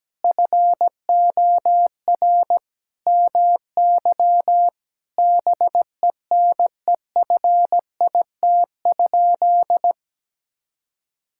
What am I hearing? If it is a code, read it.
Morse code, heard as a keyed tone, about 17 wpm, 700 Hz. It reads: FOR MY BENEFIT?
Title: Sentences from Top 1000 Words - Review Part 6 - 17wpm